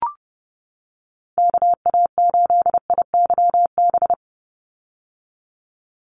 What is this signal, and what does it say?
Morse code, heard as a keyed tone, about 30 wpm, 700 Hz. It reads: KA8IYB
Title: USA Call Signs 30wpm